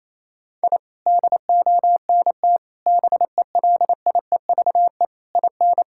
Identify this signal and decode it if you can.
Morse code, heard as a keyed tone, about 28 wpm, 700 Hz. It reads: I DONT BELIEVE IN
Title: Sentences from Top 1000 Words - Review Part 6 - 28wpm